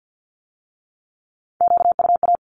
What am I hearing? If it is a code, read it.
Morse code, heard as a keyed tone, about 50 wpm, 700 Hz. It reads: QUA
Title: Q-Codes with Spoken Meaning - 50wpm